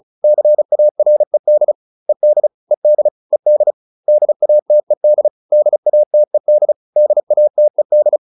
Morse code, heard as a keyed tone, about 34 wpm, 600 Hz. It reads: CARED ED ED ED DATED DATED DATED